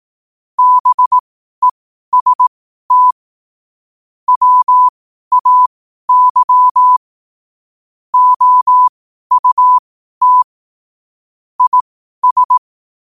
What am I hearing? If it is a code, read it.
Morse code, heard as a keyed tone, about 18 wpm, spaced as for 12 wpm, 1000 Hz. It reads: BEST WAY OUT IS
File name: qotdfiles / 12wpm